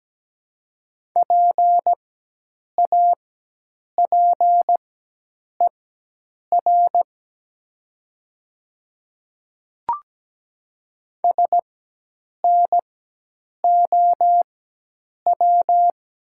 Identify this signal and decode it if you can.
Morse code, heard as a keyed tone, about 17 wpm, spaced as for 8 wpm, 700 Hz. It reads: PAPER SNOW
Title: 4x Character Spacing - Top 500 Words - 17wpm